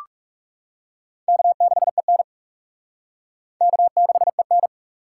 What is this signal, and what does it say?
Morse code, heard as a keyed tone, about 37 wpm, 700 Hz. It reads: K6EN K6EN
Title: Speed Racing - USA 1x2 Call Signs 30wpm